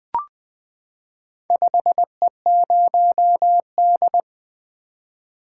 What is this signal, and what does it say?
Morse code, heard as a keyed tone, about 20 wpm, 700 Hz. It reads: 5E0D
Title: Four Letter-Number Combinations - 20wpm